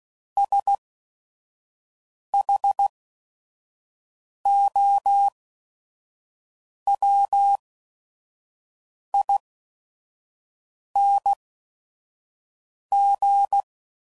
Morse code, heard as a keyed tone, about 16 wpm, spaced as for 5 wpm, 800 Hz. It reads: SHOWING
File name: Suffixes / Special ING's / VerySlowToFaster ING's